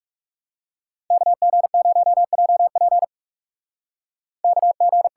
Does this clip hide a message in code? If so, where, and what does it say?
Yes: Morse code from 1.1 to 5.1 s, heard as a keyed tone, about 48 wpm, 700 Hz. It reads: KG0JP KG